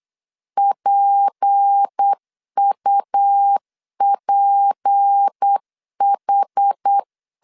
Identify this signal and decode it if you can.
Morse code, heard as a keyed tone, about 8 wpm, 800 Hz. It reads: PUPH
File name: LEZIONI / 6 PKX0 / 6 P 8wpm 3025